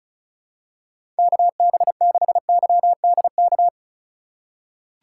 Morse code, heard as a keyed tone, about 36 wpm, 700 Hz. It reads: KB6YDK
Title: USA Call Signs - 6 Characters - 4x Character Spacing - 35wpm